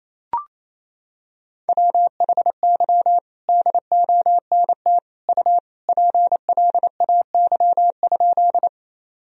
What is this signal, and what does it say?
Morse code, heard as a keyed tone, about 28 wpm, 700 Hz. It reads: WHY DONT U PLAY?
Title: Sentences from Top 300 Words and Abbreviations 28wpm